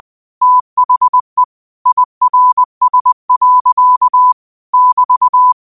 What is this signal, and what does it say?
Morse code, heard as a keyed tone, about 20 wpm, 1000 Hz. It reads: THE IRS. <BT>
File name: qotdfiles / 20wpm